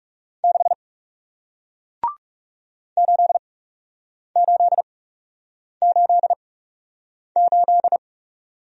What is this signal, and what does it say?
Morse code, heard as a keyed tone, about 40 wpm, spaced as for 8 wpm, 700 Hz. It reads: 6 8888